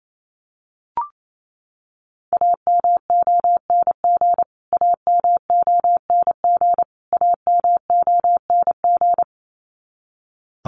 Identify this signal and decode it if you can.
Morse code, heard as a keyed tone, about 28 wpm, 700 Hz. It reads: AMONG AMONG AMONG
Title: Top 500 Words Repeated 28wpm